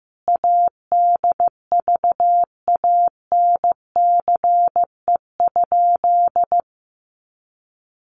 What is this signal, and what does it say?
Morse code, heard as a keyed tone, about 15 wpm, 700 Hz. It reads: ADVANCE?
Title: Sentences from Top 1400 Words 15wpm